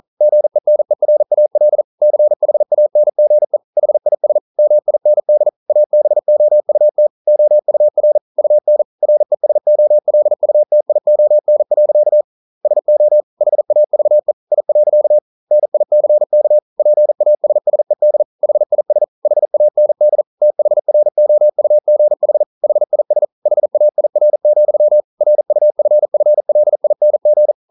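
Morse code, heard as a keyed tone, about 42 wpm, 600 Hz. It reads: GENERAL CHANGE HIS MIND ABOUT OUR UN RESOLUTION. SO HAVE I. NICK PASSED HIS HAND THROUGH HIS HAIR, RUFFLING
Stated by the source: Morse code practice at 41 wpm, from CW med Gnister.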